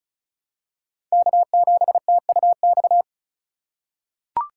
Morse code, heard as a keyed tone, about 36 wpm, 700 Hz. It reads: K7TUX
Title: USA Call Signs - 5 Characters - 2x Character Spacing - 35wpm